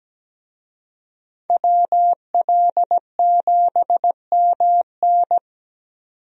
Morse code, heard as a keyed tone, about 17 wpm, 700 Hz. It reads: WL7MN
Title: USA Call Signs - 5 Characters - Repeated 3x - 17wpm